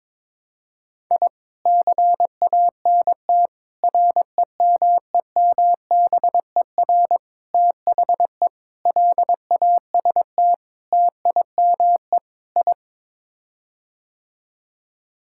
Morse code, heard as a keyed tone, about 21 wpm, 700 Hz. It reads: I CANT REMEMBER THE LAST TIME I
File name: practice / 22wpm / Sets of 7 Words - 22wpm